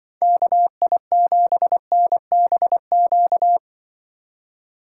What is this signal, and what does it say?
Morse code, heard as a keyed tone, about 25 wpm, 700 Hz. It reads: KI7NBQ